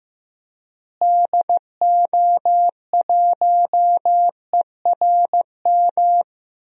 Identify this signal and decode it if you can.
Morse code, heard as a keyed tone, about 15 wpm, 700 Hz. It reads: DO1ERM
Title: DXCC-100 Call Signs - 6 Characters - Repeated 3x - 15wpm